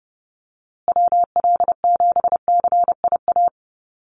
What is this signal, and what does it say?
Morse code, heard as a keyed tone, about 30 wpm, 700 Hz. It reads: WL7CIA
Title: USA Call Signs Repeated 2x 30wpm